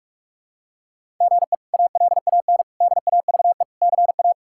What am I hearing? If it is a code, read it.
Morse code, heard as a keyed tone, about 46 wpm, 700 Hz. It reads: GE ALAN DAVE CA
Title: K1USN SST QSO Practice - 45wpm